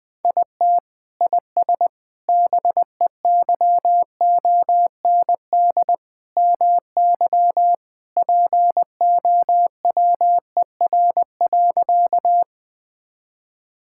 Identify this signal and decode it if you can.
Morse code, heard as a keyed tone, about 19 wpm, 700 Hz. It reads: IT IS BEYOND MY POWER.
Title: Sentences from Top 1000 Words - Review Part 6 - 20wpm